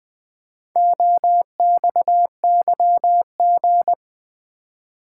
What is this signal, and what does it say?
Morse code, heard as a keyed tone, about 20 wpm, 700 Hz. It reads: OXYG
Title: Build A Word - 2 Letters At A Time - 6-Letter Words - 20wpm